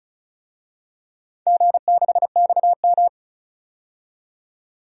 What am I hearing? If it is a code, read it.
Morse code, heard as a keyed tone, about 34 wpm, 700 Hz. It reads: G6XM
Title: DXCC-100 Call Signs - 4 Characters - 35wpm